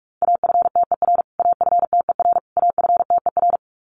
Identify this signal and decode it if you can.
Morse code, heard as a keyed tone, about 44 wpm, 700 Hz. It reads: AFTER AFTER AFTER
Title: Top 500 Words Repeated 45wpm